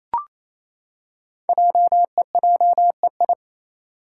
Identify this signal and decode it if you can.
Morse code, heard as a keyed tone, about 28 wpm, 700 Hz. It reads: JEJEI
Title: Code Groups - 5 Letter:Numbers - 28wpm